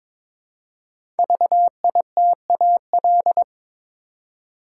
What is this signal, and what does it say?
Morse code, heard as a keyed tone, about 22 wpm, 700 Hz. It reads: VITAL
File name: 5 Letter Words - First 3 Letters Given - 22wpm